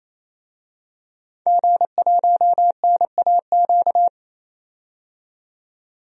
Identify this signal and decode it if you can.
Morse code, heard as a keyed tone, about 28 wpm, 700 Hz. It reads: G1NAQ
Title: DXCC-100 Call Signs - 5 Characters - 28wpm